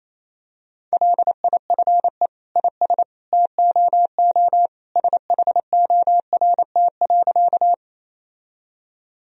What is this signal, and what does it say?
Morse code, heard as a keyed tone, about 28 wpm, 700 Hz. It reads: LIFE IS TOO SHORT.
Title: Sentences from Top 1000 Words - Review Part 2 - 28wpm